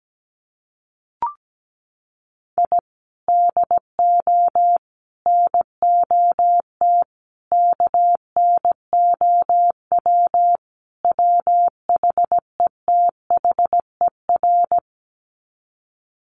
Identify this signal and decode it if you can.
Morse code, heard as a keyed tone, about 17 wpm, 700 Hz. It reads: I DO NOT KNOW WHETHER
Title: Sets of 5 Words 17wpm